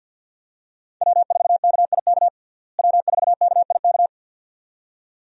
Morse code, heard as a keyed tone, about 46 wpm, 700 Hz. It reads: W4KIK W4KIK
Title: USA Call Signs - 5 Characters - Repeated 2x - 50wpm